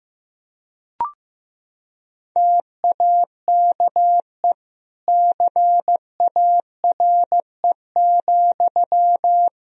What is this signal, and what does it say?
Morse code, heard as a keyed tone, about 15 wpm, 700 Hz. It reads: TAKE CARE,
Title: Sentences from Top 1000 Words - Review Part 4 - 15wpm